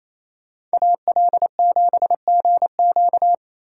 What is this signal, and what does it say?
Morse code, heard as a keyed tone, about 36 wpm, 700 Hz. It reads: AL7GQ